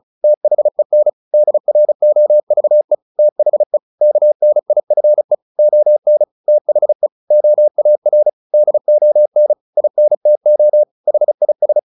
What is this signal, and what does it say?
Morse code, heard as a keyed tone, about 34 wpm, 600 Hz. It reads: THEN DROVE THE KNIFE ON THE OAR DON INTO HIS
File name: the old man and the sea 2 35wpm